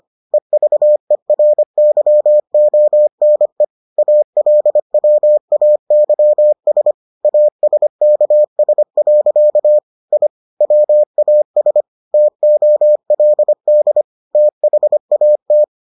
Morse code, heard as a keyed tone, about 25 wpm, 600 Hz. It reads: EVERYONE ALWAYS ASKS. I WAS TOLD THAT